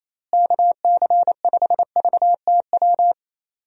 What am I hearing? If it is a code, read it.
Morse code, heard as a keyed tone, about 28 wpm, 700 Hz. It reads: KC5VTW
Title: USA Call Signs - 6 Characters - Repeated 2x - 28wpm